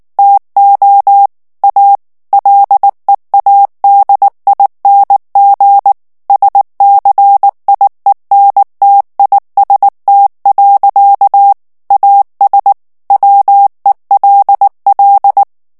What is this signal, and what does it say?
Morse code, heard as a keyed tone, about 19 wpm, 800 Hz. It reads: TO A LEADING SCIENTIST. AS WELL
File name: ANGLAIS texte clair 2 vitesse 18 mots minute